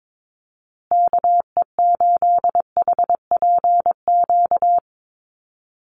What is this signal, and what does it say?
Morse code, heard as a keyed tone, about 22 wpm, 700 Hz. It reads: KE8HPQ
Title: USA Call Signs Repeated 3x 22wpm